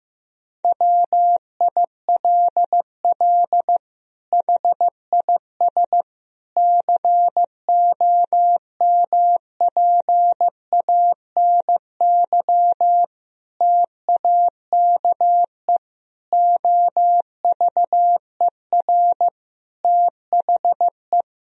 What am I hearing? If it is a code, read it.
Morse code, heard as a keyed tone, about 16 wpm, 700 Hz. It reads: WILL HIS COMPANY TAKE OVER THE
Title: Sentences from Top 1000 Words - Review Part 4 - 15wpm